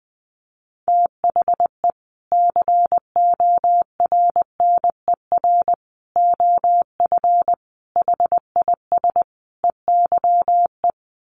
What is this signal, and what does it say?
Morse code, heard as a keyed tone, about 20 wpm, 700 Hz. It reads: THE CORNER OF HIS EYE
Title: Sets of 5 Words Encore2 20wpm